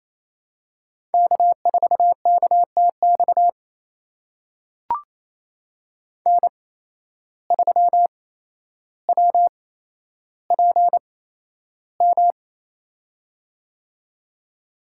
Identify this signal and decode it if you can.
Morse code, heard as a keyed tone, about 28 wpm, spaced as for 12 wpm, 700 Hz. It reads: K4KTX N3WPM
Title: USA Call Signs - 5 Characters - 8x Character Spacing - 28wpm